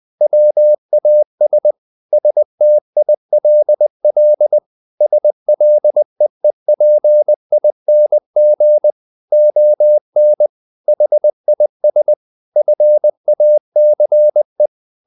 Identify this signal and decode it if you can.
Morse code, heard as a keyed tone, about 20 wpm, 600 Hz. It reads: WAS STILL SLEEPING ON HIS FACE
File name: the old man and the sea part 2 20wpm